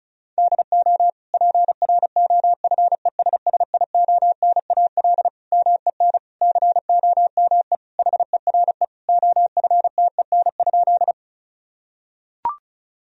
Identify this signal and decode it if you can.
Morse code, heard as a keyed tone, about 36 wpm, 700 Hz. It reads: DO PROFESSIONAL MEN COME HERE OFTEN?
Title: Sentences from Top 1000 Words - Review Part 4 - 35wpm